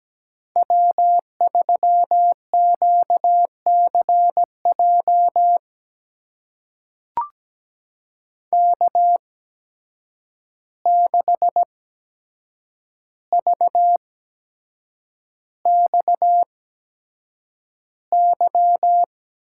Morse code, heard as a keyed tone, about 17 wpm, spaced as for 7 wpm, 700 Hz. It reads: W3QCJ K6VXY
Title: USA Call Signs - 8x Character Spacing - 17wpm